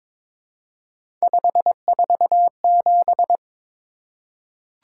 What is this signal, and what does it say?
Morse code, heard as a keyed tone, about 22 wpm, 700 Hz. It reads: 547
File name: practice / 22wpm / Three Letter-Number Combinations - 22wpm